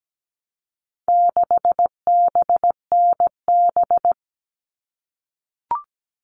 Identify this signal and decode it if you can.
Morse code, heard as a keyed tone, about 17 wpm, 700 Hz. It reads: 6BNB
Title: All Cut Numbers - 4-Digit 17wpm